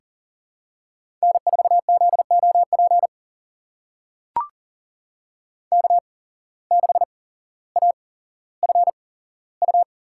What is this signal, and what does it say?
Morse code, heard as a keyed tone, about 39 wpm, spaced as for 17 wpm, 700 Hz. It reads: N4ZOP K6AFU
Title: USA Call Signs - 5 Characters - 8x Character Spacing - 40wpm